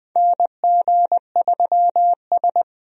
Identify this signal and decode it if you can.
Morse code, heard as a keyed tone, about 20 wpm, 700 Hz. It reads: NG3S
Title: USA Call Signs - 4 Characters - 20wpm